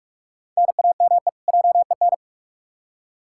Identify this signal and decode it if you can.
Morse code, heard as a keyed tone, about 41 wpm, 700 Hz. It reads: NAME JEN
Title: K1USN SST QSO Practice 45wpm